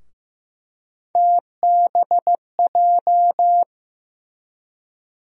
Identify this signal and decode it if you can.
Morse code, heard as a keyed tone, about 16 wpm, 700 Hz. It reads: TBJ
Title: USA Call Sign Postfixes - 3 Letters 15wpm